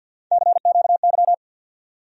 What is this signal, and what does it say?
Morse code, heard as a keyed tone, about 53 wpm, 700 Hz. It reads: CQY
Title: USA Call Sign Postfixes - 3 Letters 50wpm